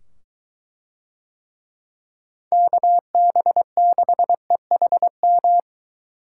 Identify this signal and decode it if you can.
Morse code, heard as a keyed tone, about 23 wpm, 700 Hz. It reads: KB6EHM